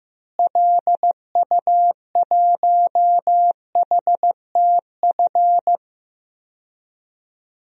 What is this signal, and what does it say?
Morse code, heard as a keyed tone, about 15 wpm, 700 Hz. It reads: LU1HTF
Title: DXCC-100 Call Signs - 6 Characters - Repeated 2x - 15wpm